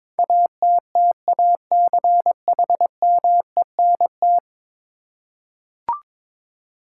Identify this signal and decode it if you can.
Morse code, heard as a keyed tone, about 21 wpm, 700 Hz. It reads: ATTACHMENT
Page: morse-code-ninja-practice-22wpm